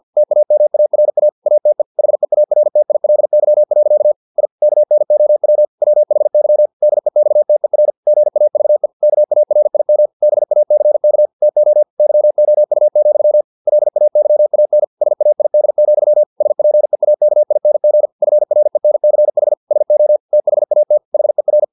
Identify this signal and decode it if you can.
Morse code, heard as a keyed tone, about 50 wpm, 600 Hz. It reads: TAMARA ATE HEARTILY. I KNOW WHY DEXTER GAVE KARIM BACK TO YOU, LAYAN SAID, SPEAKING FRENCH SO THAT HER
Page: Never 49wpm